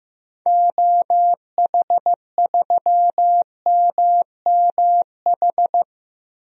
Morse code, heard as a keyed tone, about 15 wpm, 700 Hz. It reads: OH3MMH
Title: DXCC-100 Call Signs - 6 Characters - 15wpm